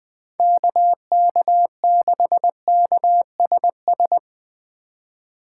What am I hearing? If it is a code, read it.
Morse code, heard as a keyed tone, about 20 wpm, 700 Hz. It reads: KK6KSS
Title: USA Call Signs - 6 Characters - 4x Character Spacing - 20wpm